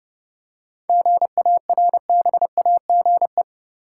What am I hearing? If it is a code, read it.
Morse code, heard as a keyed tone, about 32 wpm, 700 Hz. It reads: GARBAGE
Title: Seven Letter Words - 30wpm